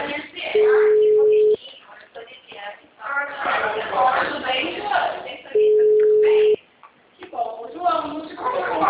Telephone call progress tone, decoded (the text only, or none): ringback tone